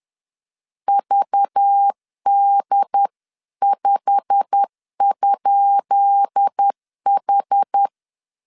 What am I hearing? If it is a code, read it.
Morse code, heard as a keyed tone, about 10 wpm, 800 Hz. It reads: VD5?H
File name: LEZIONI / 10 SIGNS / 10B INT 10wpm 2844